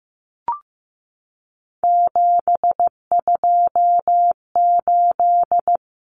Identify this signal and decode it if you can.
Morse code, heard as a keyed tone, about 15 wpm, 700 Hz. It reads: 728
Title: Common Cut Numbers - 3-Digit 15wpm